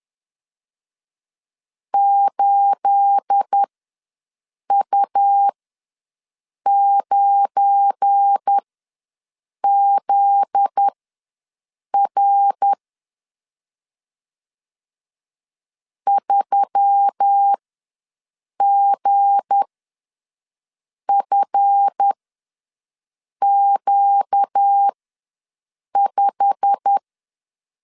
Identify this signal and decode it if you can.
Morse code, heard as a keyed tone, about 10 wpm, spaced as for 6 wpm, 800 Hz. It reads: 8U9ZR 3GFQ5
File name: LEZIONI / 9 2867 ws 10wpm 4215